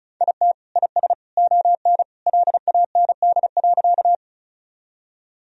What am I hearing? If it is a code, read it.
Morse code, heard as a keyed tone, about 34 wpm, 700 Hz. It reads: IT IS ON LAND.
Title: Sentences from Top 1000 Words - Review Part 1 - 35wpm